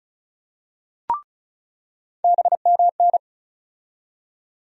Morse code, heard as a keyed tone, about 34 wpm, 700 Hz. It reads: BMN
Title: USA Call Sign Postfixes - 3 Letters 35wpm